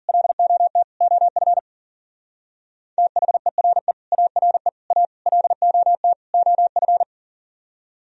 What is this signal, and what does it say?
Morse code, heard as a keyed tone, about 49 wpm, 700 Hz. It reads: LOT OF THERE ARE A LOT OF
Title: Speed-Racing - Sets of 5 Words 40wpm